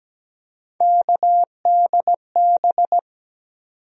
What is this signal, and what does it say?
Morse code, heard as a keyed tone, about 17 wpm, 700 Hz. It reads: KDB
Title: USA Call Sign Postfixes - 3 Letters 17wpm